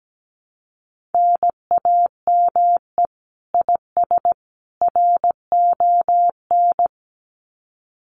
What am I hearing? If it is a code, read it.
Morse code, heard as a keyed tone, about 17 wpm, 700 Hz. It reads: NAME IS RON
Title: Common QSO Phrases 17wpm